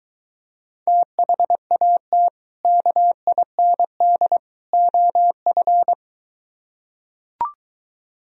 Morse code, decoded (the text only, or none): THAT KIND OF